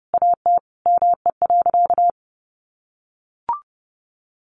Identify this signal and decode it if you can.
Morse code, heard as a keyed tone, about 28 wpm, 700 Hz. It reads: AT ME.
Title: Sentences from Top 1500 Words 30wpm